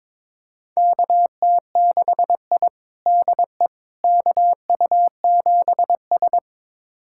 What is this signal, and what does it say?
Morse code, heard as a keyed tone, about 22 wpm, 700 Hz. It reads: KT6I DE KU7S
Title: USA Callsign DE Callsign - 4 Characters - 22wpm